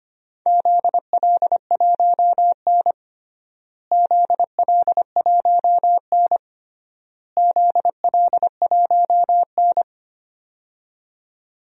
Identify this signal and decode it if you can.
Morse code, heard as a keyed tone, about 25 wpm, 700 Hz. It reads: ZL1N ZL1N ZL1N
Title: DXCC-100 Call Signs - 4 Characters - Repeated 3x - 25wpm